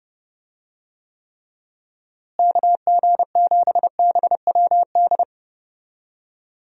Morse code, heard as a keyed tone, about 32 wpm, 700 Hz. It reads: KG7BWD
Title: USA Call Signs - 6 Characters - Repeated 3x - 30wpm